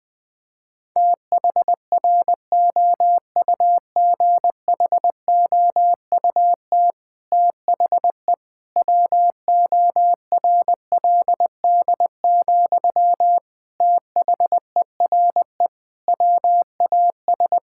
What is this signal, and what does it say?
Morse code, heard as a keyed tone, about 20 wpm, 700 Hz. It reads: THROUGHOUT THE WORLD, THERE WAS
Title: Sentences from Top 1000 Words - Review Part 5 - 20wpm